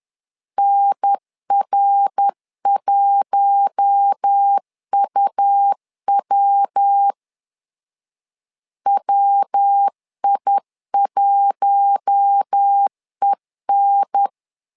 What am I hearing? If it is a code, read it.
Morse code, heard as a keyed tone, about 10 wpm, 800 Hz. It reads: NR1UW WI1EN
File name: LEZIONI / 4 Q1 10wpm 2448